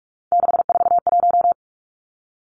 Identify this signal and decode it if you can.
Morse code, heard as a keyed tone, about 46 wpm, 700 Hz. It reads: 641
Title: Three Challenging Letter-Number Combinations 45wpm